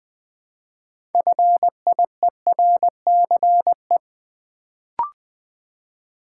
Morse code, heard as a keyed tone, about 20 wpm, 700 Hz. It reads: FIERCE
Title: Build A Word - 2 Letters At A Time - 6-Letter Words - 20wpm